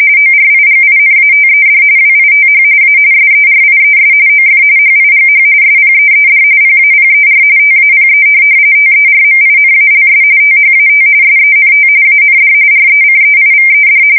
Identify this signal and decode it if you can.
RTTY, about 45.45 baud. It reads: ORIGINATED WITH A SERIES OF HIGH SPEED PAPER TAPE TERMINALS THAT SENT AND RECEIVED
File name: fsk170 / teletype-corp